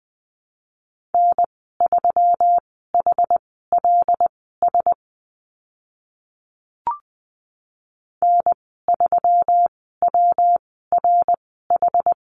Morse code, heard as a keyed tone, about 20 wpm, spaced as for 14 wpm, 700 Hz. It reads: N3HLS N3WRH
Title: 128 Lesson 21g - Number 6 - Call Signs 20wpm